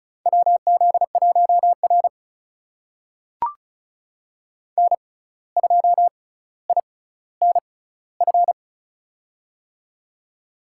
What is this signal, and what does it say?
Morse code, heard as a keyed tone, about 36 wpm, spaced as for 17 wpm, 700 Hz. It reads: WZ1R N2INF
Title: USA Call Signs - 6x Character Spacing - 35wpm